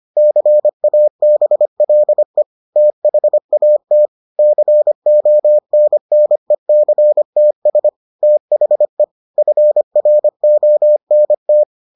Morse code, heard as a keyed tone, about 25 wpm, 600 Hz. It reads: CABLE THAT CONNECTS THE FRONT